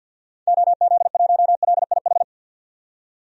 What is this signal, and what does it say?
Morse code, heard as a keyed tone, about 49 wpm, 700 Hz. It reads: OZ1LIH
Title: DXCC-100 Call Signs - 6 Characters - 50wpm